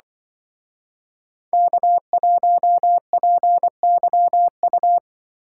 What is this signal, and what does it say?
Morse code, heard as a keyed tone, about 24 wpm, 700 Hz. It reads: K1PYU
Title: Speed Racing - USA 1x3 Call Signs 20wpm